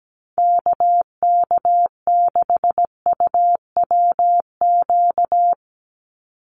Morse code, heard as a keyed tone, about 17 wpm, 700 Hz. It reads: KK6UWQ
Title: USA Call Signs 17wpm